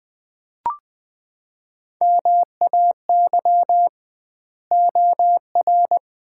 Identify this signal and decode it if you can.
Morse code, heard as a keyed tone, about 20 wpm, 700 Hz. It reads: MAY OR
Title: 2x Word-Spacing - Sets of 5 Words - 20wpm